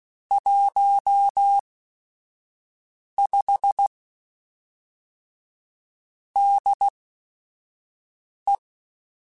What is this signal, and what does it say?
Morse code, heard as a keyed tone, about 16 wpm, spaced as for 5 wpm, 800 Hz. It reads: 15 DE